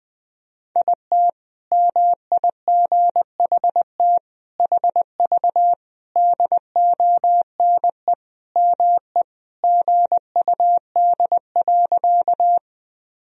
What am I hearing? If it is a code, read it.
Morse code, heard as a keyed tone, about 20 wpm, 700 Hz. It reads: IT MIGHT HV DONE ME GUD.
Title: Sentences from Top 500 Words and Abbreviations 20wpm